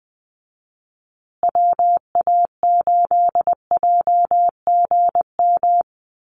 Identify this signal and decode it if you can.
Morse code, heard as a keyed tone, about 20 wpm, 700 Hz. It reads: WA8JGM
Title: USA Call Signs Repeated 2x 20wpm